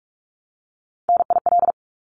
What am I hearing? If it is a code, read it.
Morse code, heard as a keyed tone, about 46 wpm, 700 Hz. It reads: NIL
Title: Abbreviations 45wpm